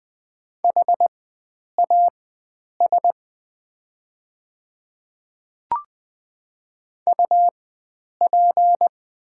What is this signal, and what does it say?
Morse code, heard as a keyed tone, about 20 wpm, spaced as for 9 wpm, 690 Hz. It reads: HAS UP
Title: 4x Character Spacing - Top 100 Words - 20wpm